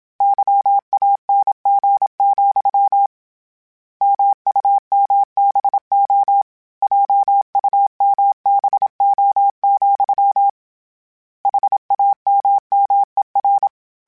Morse code, heard as a keyed tone, about 27 wpm, 800 Hz. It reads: YANG, MUMBO JUMBO, HAMMER
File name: Misc / Misc-mp3s / Word Pairs @25